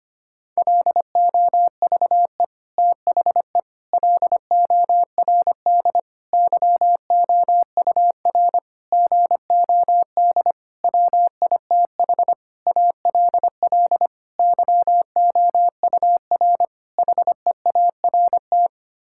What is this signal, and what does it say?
Morse code, heard as a keyed tone, about 25 wpm, 700 Hz. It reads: LOVE THE LORD YOUR GOD WITH ALL YOUR HEART
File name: Sets of 9 Words - 25wpm